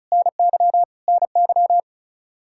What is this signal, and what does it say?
Morse code, heard as a keyed tone, about 34 wpm, 700 Hz. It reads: NY NY
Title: K1USN SST QSO Practice 35wpm